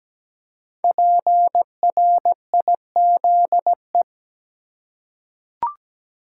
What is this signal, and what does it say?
Morse code, heard as a keyed tone, about 17 wpm, 700 Hz. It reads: PRIZE
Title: Five Letter Words - 17wpm